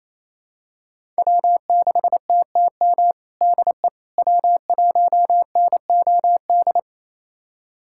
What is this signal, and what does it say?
Morse code, heard as a keyed tone, about 28 wpm, 700 Hz. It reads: W6TTM DE W1NOD
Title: USA Callsign DE Callsign - 5 Characters - 28wpm